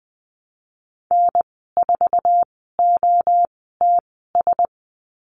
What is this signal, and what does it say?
Morse code, heard as a keyed tone, about 20 wpm, spaced as for 15 wpm, 700 Hz. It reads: N4OTS
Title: 055 Lesson 10g - Letter r - Call Signs 20wpm